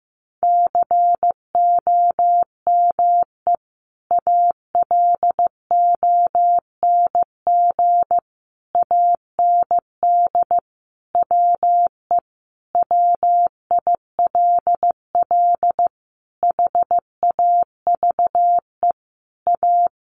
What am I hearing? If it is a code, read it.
Morse code, heard as a keyed tone, about 15 wpm, 700 Hz. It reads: COME ALONG AND WE WILL HAVE A
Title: Sentences from Top 1500 Words 15wpm